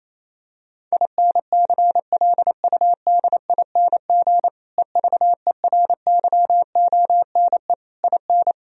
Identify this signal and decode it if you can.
Morse code, heard as a keyed tone, about 28 wpm, 700 Hz. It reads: INCLUDING EVERYONE IN
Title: Sentences from Top 1000 Words - Review Part 4 - 28wpm